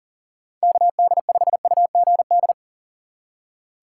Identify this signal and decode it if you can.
Morse code, heard as a keyed tone, about 40 wpm, 700 Hz. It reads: KD5UGD
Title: USA Call Signs - 4x Character Spacing - 40wpm